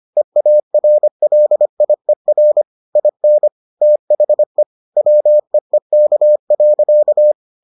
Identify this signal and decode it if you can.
Morse code, heard as a keyed tone, about 25 wpm, 600 Hz. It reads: EARLIER IN THE WEEK.